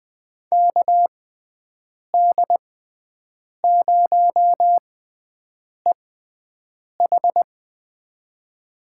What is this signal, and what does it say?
Morse code, heard as a keyed tone, about 21 wpm, spaced as for 7 wpm, 700 Hz. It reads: KD0EH
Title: USA Call Signs - 5 Characters - 6x Character Spacing - 20wpm